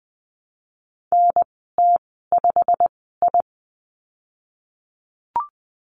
Morse code, heard as a keyed tone, about 20 wpm, spaced as for 15 wpm, 700 Hz. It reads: NT5I
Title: 088 Lesson 15g - Number 5 - Call Signs 20wpm